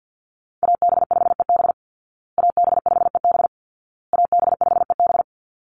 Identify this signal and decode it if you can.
Morse code, heard as a keyed tone, about 50 wpm, 700 Hz. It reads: AB5EB AB5EB AB5EB
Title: Call Signs Repeated 2x - 50wpm